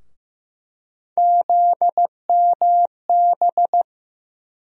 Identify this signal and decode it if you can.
Morse code, heard as a keyed tone, about 15 wpm, 700 Hz. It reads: ZMB
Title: USA Call Sign Postfixes - 3 Letters 15wpm